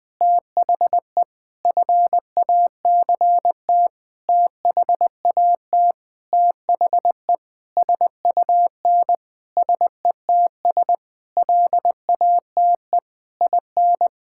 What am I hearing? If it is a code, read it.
Morse code, heard as a keyed tone, about 19 wpm, 700 Hz. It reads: THE FACT THAT THE SUN SETS LATE IN
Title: Sentences with 8-Word Phrases - Spoken Beginning - 20wpm